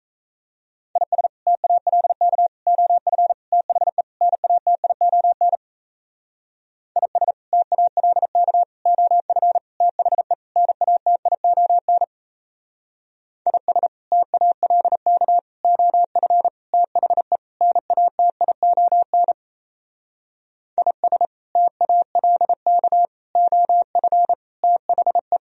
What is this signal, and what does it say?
Morse code, heard as a keyed tone, about 36 wpm, 700 Hz. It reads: IS TALK OF THE NATION IS TALK OF THE NATION IS TALK OF THE NATION IS TALK OF THE